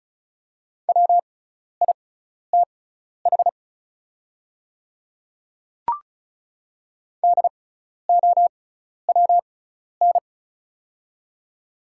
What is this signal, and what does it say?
Morse code, heard as a keyed tone, about 34 wpm, spaced as for 12 wpm, 700 Hz. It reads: WITH DOWN
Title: 6x Character Spacing - Top 100 Words - 35wpm